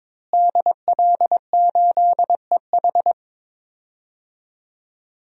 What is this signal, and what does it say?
Morse code, heard as a keyed tone, about 22 wpm, 700 Hz. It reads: DL8EH
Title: DXCC-100 Call Signs - 5 Characters - Repeated 2x - 22wpm